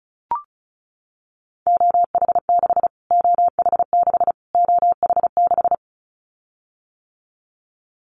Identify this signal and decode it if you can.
Morse code, heard as a keyed tone, about 34 wpm, 700 Hz. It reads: OH6 OH6 OH6
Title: Callsign Prefixes for Top 25 DXCC - Repeated 3x - 35wpm